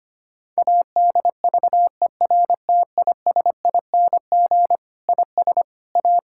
Morse code, heard as a keyed tone, about 25 wpm, 700 Hz. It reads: ADVERTISING IS A
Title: Sentences from Top 1000 Words - Review Part 6 - 25wpm